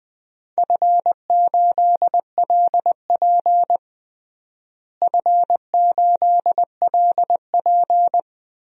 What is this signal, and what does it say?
Morse code, heard as a keyed tone, about 21 wpm, 700 Hz. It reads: F8LP F8LP
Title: DXCC-100 Call Signs - 4 Characters - Repeated 2x - 20wpm